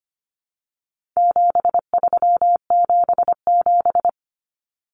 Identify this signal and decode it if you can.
Morse code, heard as a keyed tone, about 25 wpm, 700 Hz. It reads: 7377
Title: Common Cut Numbers - 4-Digit 25wpm